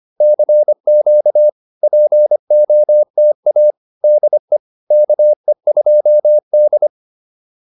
Morse code, heard as a keyed tone, about 25 wpm, 600 Hz. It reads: CQ POTA DE KE2D